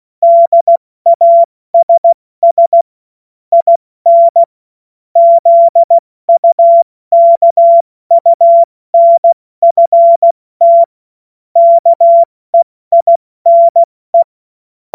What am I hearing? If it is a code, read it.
Morse code, heard as a keyed tone, about 16 wpm, 680 Hz. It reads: DASS IN ZUKUNFT KEINE
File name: Witz-28-Oktober-2025-70-F80